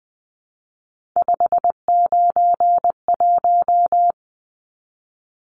Morse code, heard as a keyed tone, about 20 wpm, 700 Hz. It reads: 591
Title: Numbers - 3-Digit 20wpm